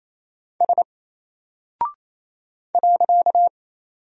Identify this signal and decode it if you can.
Morse code, heard as a keyed tone, about 27 wpm, 700 Hz. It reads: S .
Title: Single Character (Letter:Number:Punctuation:Prosign) 28wpm